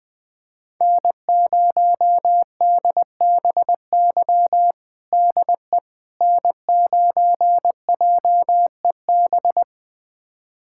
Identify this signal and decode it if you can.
Morse code, heard as a keyed tone, about 19 wpm, 700 Hz. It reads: N0DBY DE N9JEB
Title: USA Callsign DE Callsign 20wpm